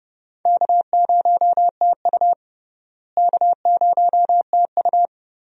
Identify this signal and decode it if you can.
Morse code, heard as a keyed tone, about 30 wpm, 700 Hz. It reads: K0TU K0TU
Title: USA Call Signs - 4 Characters - Repeated 2x - 30wpm